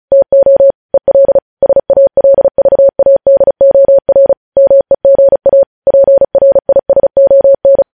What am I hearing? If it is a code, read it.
Morse code, heard as a keyed tone, about 36 wpm, 550 Hz. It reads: TO EL SALVADOR MEGA PRISON
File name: podcast / News headlines-1x-35x35